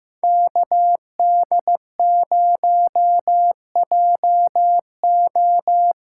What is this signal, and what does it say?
Morse code, heard as a keyed tone, about 15 wpm, 700 Hz. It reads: KD0JO
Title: USA Call Signs - 6 Characters - 8x Character Spacing - 15wpm